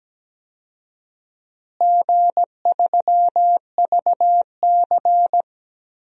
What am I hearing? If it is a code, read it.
Morse code, heard as a keyed tone, about 17 wpm, 700 Hz. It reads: G3VC
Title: DXCC-100 Call Signs - 4 Characters - Repeated 3x - 17wpm